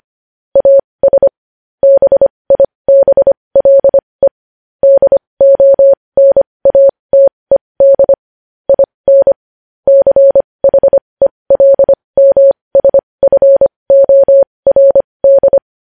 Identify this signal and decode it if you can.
Morse code, heard as a keyed tone, about 25 wpm, spaced as for 20 wpm, 550 Hz. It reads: AS BIBLE DONATED IN CHELMSFORD
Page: news-headlines-in-morse-code-at-20-wpm